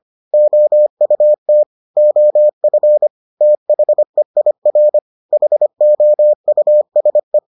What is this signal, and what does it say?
Morse code, heard as a keyed tone, about 25 wpm, 600 Hz. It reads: OUT OF THEIR HOUSE